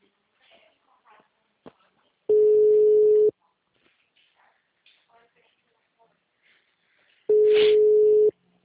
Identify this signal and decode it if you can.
Ringback tone (Europe, 425 Hz), with speech.